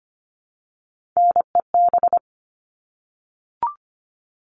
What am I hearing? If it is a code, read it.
Morse code, heard as a keyed tone, about 25 wpm, 700 Hz. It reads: NEB
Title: All Cut Numbers - 3-Digit 25wpm